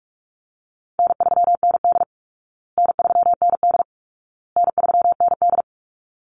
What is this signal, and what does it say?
Morse code, heard as a keyed tone, about 44 wpm, 700 Hz. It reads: N3ND N3ND N3ND
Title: Easy Call Signs Repeated 3x - 45wpm